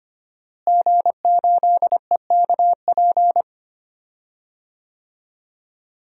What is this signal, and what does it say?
Morse code, heard as a keyed tone, about 25 wpm, 700 Hz. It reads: G8EKP
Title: DXCC-100 Call Signs - 5 Characters - Repeated 3x - 25wpm